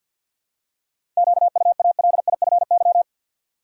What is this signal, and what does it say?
Morse code, heard as a keyed tone, about 49 wpm, 700 Hz. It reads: QUALIFY
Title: Seven Letter Words - 50wpm